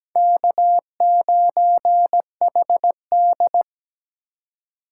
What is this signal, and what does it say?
Morse code, heard as a keyed tone, about 20 wpm, 700 Hz. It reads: K9HD
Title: Speed Racing - USA 1x2 Call Signs 17wpm